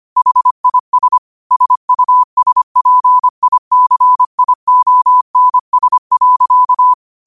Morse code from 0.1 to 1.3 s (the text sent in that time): HIS